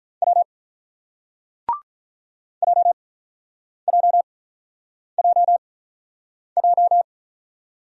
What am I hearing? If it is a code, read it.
Morse code, heard as a keyed tone, about 46 wpm, spaced as for 9 wpm, 690 Hz. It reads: W JJJJ